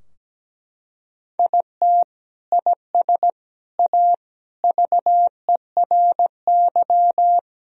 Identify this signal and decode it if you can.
Morse code, heard as a keyed tone, about 17 wpm, 700 Hz. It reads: IT IS A VERY
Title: Sentences from Top 1000 Words - Review Part 4 - 17wpm